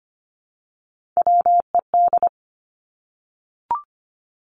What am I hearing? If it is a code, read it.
Morse code, heard as a keyed tone, about 25 wpm, 700 Hz. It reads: WED
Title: Three Challenging Letter-Number Combinations 25wpm